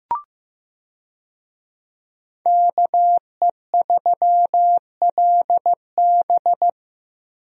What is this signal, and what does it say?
Morse code, heard as a keyed tone, about 15 wpm, 700 Hz. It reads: KE3LB
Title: USA Call Signs - 5 Characters - Repeated 3x - 15wpm